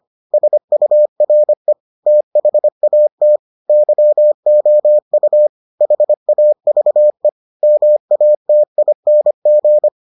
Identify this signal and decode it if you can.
Morse code, heard as a keyed tone, about 25 wpm, 600 Hz. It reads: SURE THAT YOU HAVE MATING